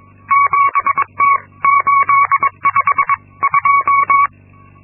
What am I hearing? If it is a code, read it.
Morse code, heard as a keyed tone, about 22 wpm, spaced as for 19 wpm, 1120 Hz. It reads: 7T852
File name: Linksmakalnis-2